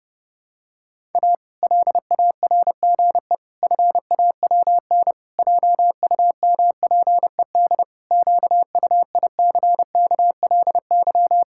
Morse code, heard as a keyed tone, about 30 wpm, 700 Hz. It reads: A LARGE FAWN JUMPED QUICKLY
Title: Pangram Challenge v2 - 30wpm